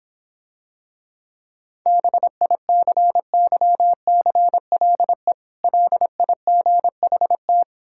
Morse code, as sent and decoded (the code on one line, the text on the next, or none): -... .. -.-. -.-- -.-. .-.. / .-.. .. --. .... -
BICYCLE LIGHT